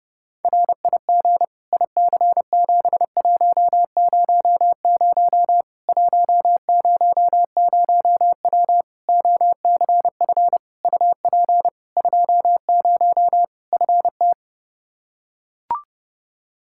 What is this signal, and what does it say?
Morse code, heard as a keyed tone, about 30 wpm, 700 Hz. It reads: RIG IC7100 100W OCF UP 20 FT
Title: Practice QSO - Exchange 2 - Info - 30wpm